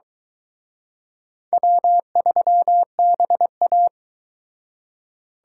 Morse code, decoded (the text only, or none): W3BA